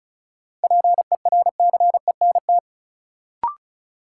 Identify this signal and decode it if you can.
Morse code, heard as a keyed tone, about 36 wpm, 700 Hz. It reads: PERCENT
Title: Sets of 2 Words - New Word Drill - 35wpm